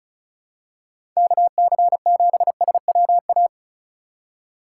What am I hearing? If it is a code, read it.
Morse code, heard as a keyed tone, about 36 wpm, 700 Hz. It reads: KC7SWA
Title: USA Call Signs - 6 Characters - 8x Character Spacing - 35wpm